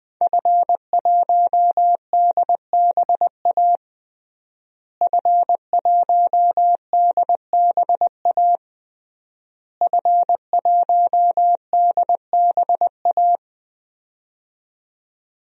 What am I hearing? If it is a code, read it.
Morse code, heard as a keyed tone, about 20 wpm, 700 Hz. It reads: F1DBA F1DBA F1DBA
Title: DXCC-100 Call Signs - 5 Characters - Repeated 3x - 20wpm